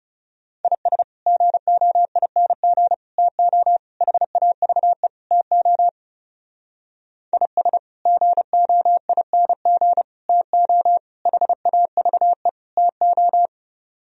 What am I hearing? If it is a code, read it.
Morse code, heard as a keyed tone, about 38 wpm, 700 Hz. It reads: IS GOING TO HAVE TO IS GOING TO HAVE TO